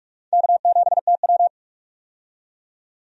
Morse code, heard as a keyed tone, about 46 wpm, 700 Hz. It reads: K7TW
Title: USA Call Signs - 4 Characters - 45wpm